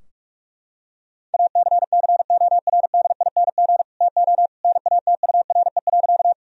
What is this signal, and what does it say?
Morse code, heard as a keyed tone, about 46 wpm, 700 Hz. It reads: ACCORDING TO NATURE.
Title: Sentences from Top 1000 Words - Review Part 4 - 45wpm